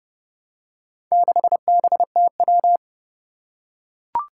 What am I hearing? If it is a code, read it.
Morse code, heard as a keyed tone, about 30 wpm, 700 Hz. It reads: 6BTW
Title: Four Challenging Letter-Number Combinations 30wpm